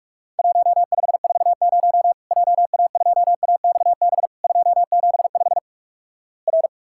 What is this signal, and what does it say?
Morse code, heard as a keyed tone, about 46 wpm, 700 Hz. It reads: 1540 JA2AXB 275 R
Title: Worked All Europe DX Contest - QTC Exchange - 45wpm